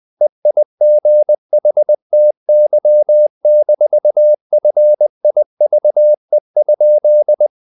Morse code, heard as a keyed tone, about 20 wpm, 600 Hz. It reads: EIGHTY-FIVE?
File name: the old man and the sea / the old man and the sea part 1 20wpm